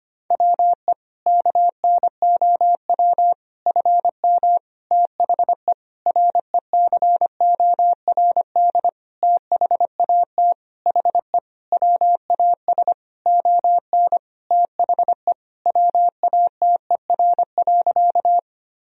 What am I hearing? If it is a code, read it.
Morse code, heard as a keyed tone, about 25 wpm, 700 Hz. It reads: WE KNOW FM THE RECORD THAT HE WAS ON THE WATER.
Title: Sentences from Top 500 Words and Abbreviations 25wpm